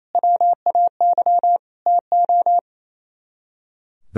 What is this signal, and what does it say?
Morse code, heard as a keyed tone, about 35 wpm, 700 Hz. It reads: WAY TO